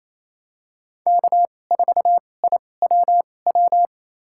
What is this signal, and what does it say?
Morse code, heard as a keyed tone, about 28 wpm, spaced as for 20 wpm, 700 Hz. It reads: K4IWW
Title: USA Call Signs - 5 Characters - 2x Character Spacing - 28wpm